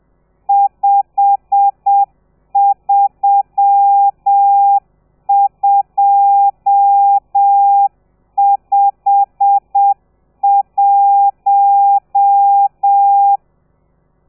Morse code, heard as a keyed tone, about 7 wpm, 800 Hz. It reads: 53251
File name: telegrafia / i2tfj / Lez12 30